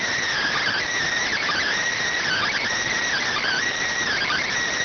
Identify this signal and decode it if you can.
Emergency Alert System (SAME) tones, heard as harsh buzzing bursts. It is heard from 0.3 to 4.5 s.